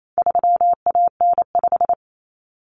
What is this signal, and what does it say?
Morse code, heard as a keyed tone, about 28 wpm, 700 Hz. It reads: 3AN5
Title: Common Cut Numbers - 4-Digit 28wpm